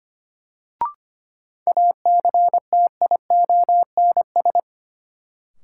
Morse code, heard as a keyed tone, about 25 wpm, 700 Hz. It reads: ACTIONS
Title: Sentences from Top 1000 Words - Review Part 6 - New Word Drill - 25wpm